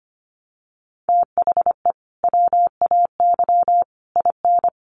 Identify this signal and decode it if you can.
Morse code, heard as a keyed tone, about 25 wpm, 700 Hz. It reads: THE WAY IN
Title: Sets of 5 Words Encore2 25wpm